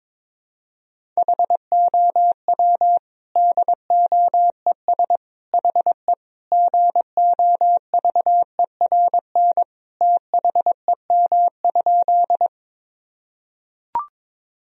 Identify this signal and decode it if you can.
Morse code, heard as a keyed tone, about 22 wpm, 700 Hz. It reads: HOW DOES HE GOVERN THEM?
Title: Sentences from Top 1000 Words - Review Part 2 - 22wpm